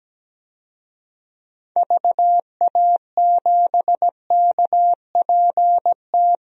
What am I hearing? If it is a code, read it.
Morse code, heard as a keyed tone, about 17 wpm, 700 Hz. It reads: VA7KPT
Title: DXCC-100 Call Signs - 6 Characters - Repeated 2x - 17wpm